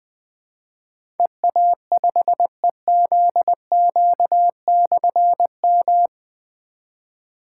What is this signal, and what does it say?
Morse code, heard as a keyed tone, about 20 wpm, 700 Hz. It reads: EA5EZQ/M
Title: Difficult Call Signs 20wpm